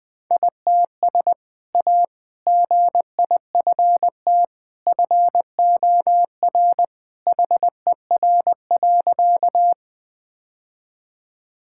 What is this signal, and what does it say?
Morse code, heard as a keyed tone, about 20 wpm, 700 Hz. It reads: ITS A GIFT FOR HER.
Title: Sentences with 2-Word Phrases - Spoken Beginning - 20wpm